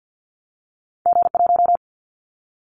Morse code, heard as a keyed tone, about 50 wpm, 700 Hz. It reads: G1
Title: Callsign Prefixes for Top 25 DXCC - Repeated 3x - 50wpm